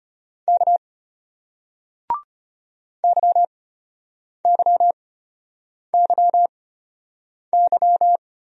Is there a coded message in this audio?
Yes: Morse code from 0.5 to 8.2 s, heard as a keyed tone, about 34 wpm, spaced as for 8 wpm, 700 Hz. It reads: K YYYY